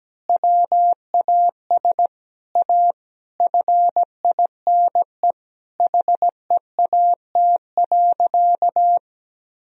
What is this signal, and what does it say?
Morse code, heard as a keyed tone, about 17 wpm, 700 Hz. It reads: WAS A FINE HEAT.
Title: Sentences from Top 1000 Words - Review Part 3 - 17wpm